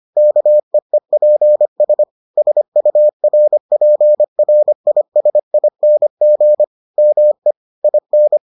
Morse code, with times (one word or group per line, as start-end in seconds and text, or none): 0.2-2.0 KEEPS
2.4-6.6 SURPRISING
7.0-7.5 ME
7.8-8.4 IN